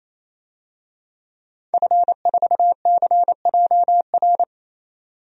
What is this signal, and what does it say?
Morse code, heard as a keyed tone, about 28 wpm, 700 Hz. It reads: F4CJR
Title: DXCC-100 Call Signs - 5 Characters - 28wpm